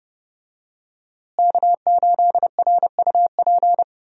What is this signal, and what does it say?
Morse code, heard as a keyed tone, about 30 wpm, 700 Hz. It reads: K8RUP
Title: USA Call Signs - 5 Characters - Repeated 3x - 30wpm